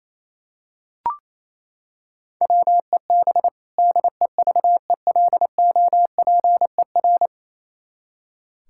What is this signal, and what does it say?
Morse code, heard as a keyed tone, about 28 wpm, 700 Hz. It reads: WEB DEVELOPER
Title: Practice QSO - Exchange 3 - More Info - 28wpm